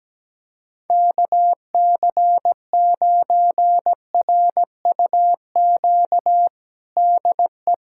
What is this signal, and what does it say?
Morse code, heard as a keyed tone, about 17 wpm, 700 Hz. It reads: KC9RUQ DE
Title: USA Callsign DE Callsign - 6 Characters - 17wpm